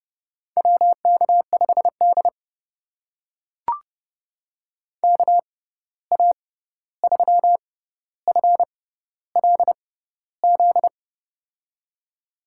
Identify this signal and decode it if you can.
Morse code, heard as a keyed tone, about 29 wpm, spaced as for 14 wpm, 700 Hz. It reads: WK5D KA3FLZ
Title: USA Call Signs - 6x Character Spacing - 30wpm